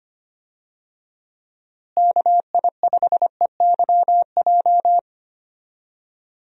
Morse code, heard as a keyed tone, about 25 wpm, 700 Hz. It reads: KI5EYJ
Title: USA Call Signs - 6 Characters - Repeated 2x - 25wpm